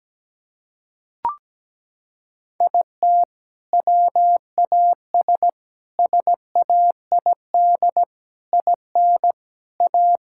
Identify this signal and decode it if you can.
Morse code, heard as a keyed tone, about 17 wpm, 700 Hz. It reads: IT WAS SAID IN A
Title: Sentences from Top 1600 Words 17wpm